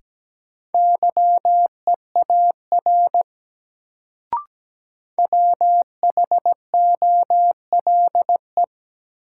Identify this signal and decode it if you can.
Morse code, heard as a keyed tone, about 17 wpm, 700 Hz. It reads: YEAR WHOLE
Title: Sets of 2 Words - New Word Drill - 17wpm